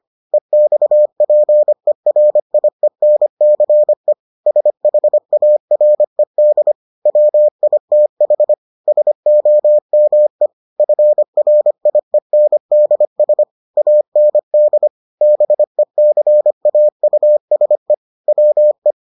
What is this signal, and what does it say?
Morse code, heard as a keyed tone, about 25 wpm, 600 Hz. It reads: EXPERIENCE SHARED WITH SOME FRIENDS AND BECAUSE WE